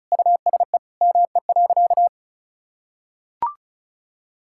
Morse code, heard as a keyed tone, about 36 wpm, 700 Hz. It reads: USE ME.
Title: Sentences from Top 1000 Words - Review Part 6 - 35wpm